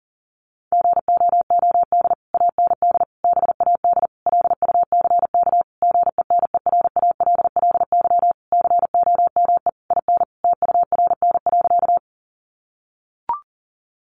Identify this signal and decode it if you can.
Morse code, heard as a keyed tone, about 40 wpm, 700 Hz. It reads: GOOD AND BAD LUCK GENERALLY COME IN TURN.
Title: Sentences from Top 1500 Words 40wpm